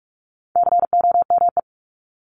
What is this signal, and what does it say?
Morse code, heard as a keyed tone, about 44 wpm, 700 Hz. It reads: COME
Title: Top 300 Words 45wpm